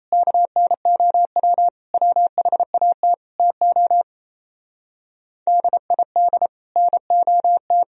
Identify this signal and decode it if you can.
Morse code, heard as a keyed tone, about 38 wpm, 700 Hz. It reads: KNOW WHAT TO DID NOT